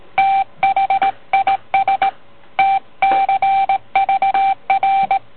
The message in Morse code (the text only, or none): THIS TCVR